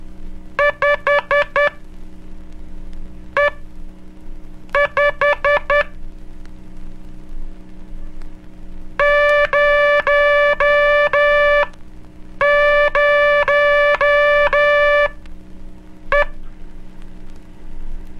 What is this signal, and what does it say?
Morse code, heard as a keyed tone, about 12 wpm, spaced as for 6 wpm, 580 Hz. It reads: 5E5 00E
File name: cursos / Curso CW mp3 / CW Aula - 01 - (5, zero, E)